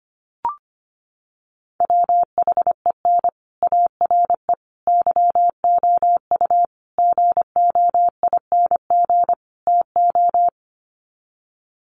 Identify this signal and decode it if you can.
Morse code, heard as a keyed tone, about 25 wpm, 700 Hz. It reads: WHEN ARE YOU GOING TO
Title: Sets of 5 Words Encore2 25wpm